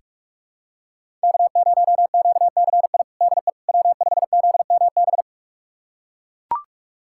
Morse code, heard as a keyed tone, about 41 wpm, 700 Hz. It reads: K0QCI DE W5ZMB
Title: USA Callsign DE Callsign - 5 Characters - 45wpm